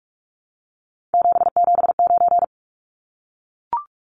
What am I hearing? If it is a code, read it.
Morse code, heard as a keyed tone, about 44 wpm, 700 Hz. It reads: 779
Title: Numbers - 3-Digit 45wpm